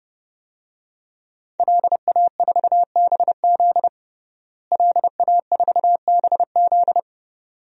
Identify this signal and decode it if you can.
Morse code, heard as a keyed tone, about 31 wpm, 700 Hz. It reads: LA4BZ LA4BZ
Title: DXCC-100 Call Signs - 5 Characters - Repeated 2x - 30wpm